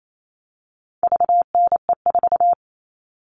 Morse code, heard as a keyed tone, about 28 wpm, 700 Hz. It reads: VNE4
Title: All Cut Numbers - 4-Digit 28wpm